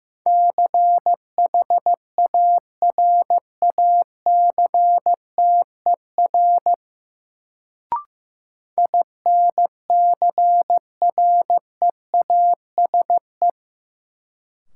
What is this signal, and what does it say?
Morse code, heard as a keyed tone, about 15 wpm, 700 Hz. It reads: CHARACTER INCREASE
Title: Sentences from Top 1200 Words - New Word Drill Alt - 15wpm